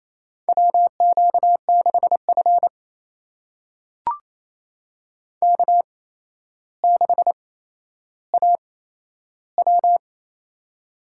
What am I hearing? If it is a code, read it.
Morse code, heard as a keyed tone, about 28 wpm, spaced as for 12 wpm, 700 Hz. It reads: WQ6F K6AW
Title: USA Call Signs - 4 Characters - 8x Character Spacing - 28wpm